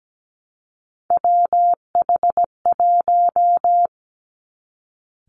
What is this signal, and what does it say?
Morse code, heard as a keyed tone, about 17 wpm, 700 Hz. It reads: WH1
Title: Three Challenging Letter-Number Combinations 17wpm